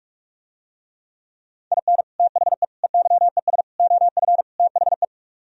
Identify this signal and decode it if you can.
Morse code, heard as a keyed tone, about 49 wpm, 700 Hz. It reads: IN THE EYES OF THE